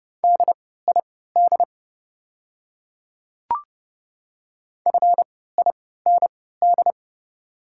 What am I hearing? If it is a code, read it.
Morse code, heard as a keyed tone, about 30 wpm, spaced as for 17 wpm, 700 Hz. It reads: DID FIND
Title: 3x Character Spacing - Top 100 Words - 30wpm